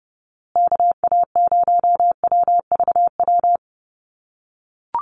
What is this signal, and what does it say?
Morse code, heard as a keyed tone, about 30 wpm, 700 Hz. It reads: KA0WVW
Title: USA Call Signs Repeated 3x 30wpm